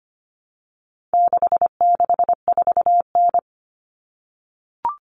Morse code, heard as a keyed tone, about 25 wpm, 700 Hz. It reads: 664N
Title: Common Cut Numbers - 4-Digit 25wpm